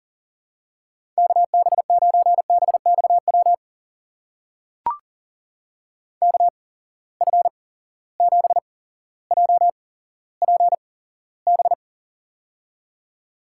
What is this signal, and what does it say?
Morse code, heard as a keyed tone, about 40 wpm, spaced as for 17 wpm, 700 Hz. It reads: KB9BXW KF7JPB
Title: USA Call Signs - 6 Characters - 8x Character Spacing - 40wpm